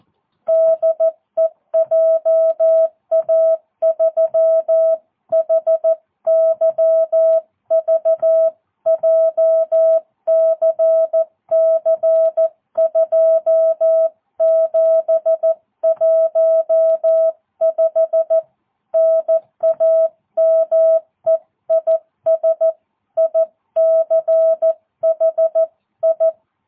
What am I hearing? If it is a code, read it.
Morse code, heard as a keyed tone, about 14 wpm, 660 Hz. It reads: DE JA3HYV JCC2715 NAME IS ICHI
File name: jh8cbh / 2022cw